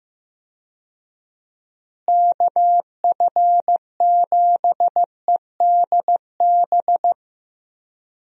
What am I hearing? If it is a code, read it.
Morse code, heard as a keyed tone, about 15 wpm, 700 Hz. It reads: KF7EDB
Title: USA Call Signs - 6 Characters - Repeated 2x - 15wpm